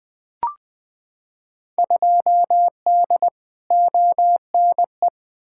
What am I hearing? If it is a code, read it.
Morse code, heard as a keyed tone, about 20 wpm, 700 Hz. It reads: 2D ONE
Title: Field Day Exchanges - 20wpm